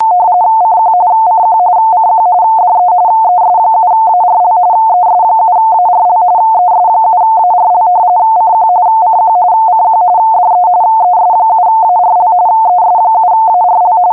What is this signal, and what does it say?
RTTY, about 45.45 baud. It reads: CQ CQ CQ DE ES7ARL ES7ARL ES7ARL CQ CQ CQ DE ES7ARL ES7ARL